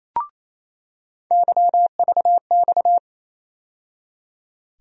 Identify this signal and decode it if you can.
Morse code, heard as a keyed tone, about 28 wpm, 700 Hz. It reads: YVX
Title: USA Call Sign Postfixes - 3 Letters 28wpm